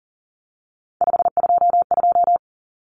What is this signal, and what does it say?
Morse code, heard as a keyed tone, about 40 wpm, 700 Hz. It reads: 522
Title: Numbers - 3-Digit 40wpm